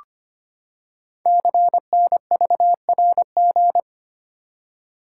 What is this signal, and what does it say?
Morse code, heard as a keyed tone, about 25 wpm, 700 Hz. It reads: CNVRG
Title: Code Groups - 5 Letters - 25wpm